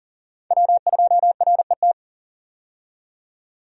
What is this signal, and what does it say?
Morse code, heard as a keyed tone, about 40 wpm, 700 Hz. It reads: W2RET
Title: USA Call Signs - 5 Characters - 40wpm